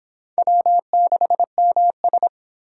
Morse code, heard as a keyed tone, about 25 wpm, 700 Hz. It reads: W6MS